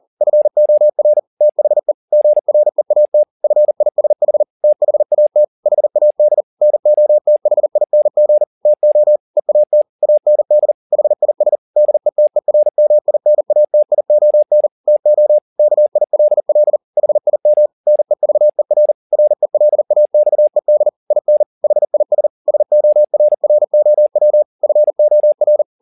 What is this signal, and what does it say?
Morse code, heard as a keyed tone, about 40 wpm, 600 Hz. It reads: FOR THE GREAT FISH THAT HAD NOTHING TO EAT AND HIS DETERMINATION TO KILL HIM NEVER RELAXED IN HIS SORROW FOR
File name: the old man and the sea 2 40wpm